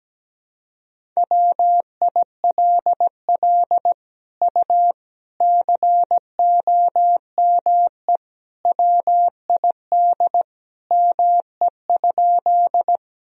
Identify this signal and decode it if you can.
Morse code, heard as a keyed tone, about 17 wpm, 700 Hz. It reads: WILL U COME WID ME?
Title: Sentences from Top 200 Words and Abbreviations 17wpm